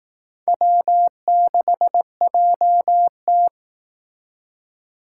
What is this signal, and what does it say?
Morse code, heard as a keyed tone, about 18 wpm, 700 Hz. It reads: W6JT
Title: Speed Racing - USA 1x2 Call Signs 15wpm